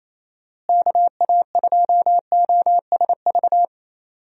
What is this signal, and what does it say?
Morse code, heard as a keyed tone, about 28 wpm, 700 Hz. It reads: KA2OSV
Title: USA Call Signs - 6 Characters - Repeated 2x - 28wpm